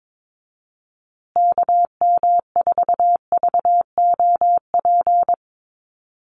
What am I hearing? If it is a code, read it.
Morse code, heard as a keyed tone, about 22 wpm, 700 Hz. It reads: KM4VOP
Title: USA Call Signs Repeated 2x 22wpm